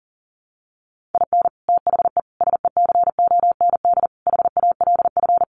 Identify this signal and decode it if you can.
Morse code, heard as a keyed tone, about 40 wpm, 700 Hz. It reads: IN THE SECOND HALF
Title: Sets of 5 Words 40wpm